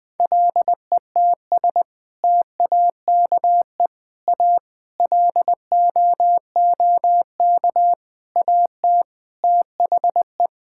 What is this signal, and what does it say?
Morse code, heard as a keyed tone, about 20 wpm, 700 Hz. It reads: LETS TAKE A LOOK AT THE
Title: Sentences with 6-Word Phrases - 20wpm